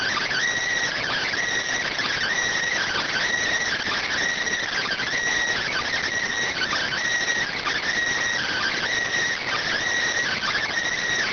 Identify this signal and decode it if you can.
Emergency Alert System (SAME) tones (header bursts), heard as harsh buzzing bursts. It is heard from 1.0 to 10.6 s.